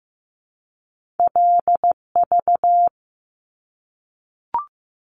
Morse code, heard as a keyed tone, about 15 wpm, 700 Hz. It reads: LV
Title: Top Two Letter Combinations in Words 15wpm